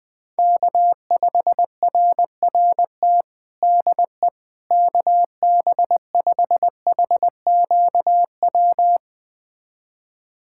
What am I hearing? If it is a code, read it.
Morse code, heard as a keyed tone, about 20 wpm, 700 Hz. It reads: K5RRT DE KB5HQW
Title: USA Callsign DE Callsign 20wpm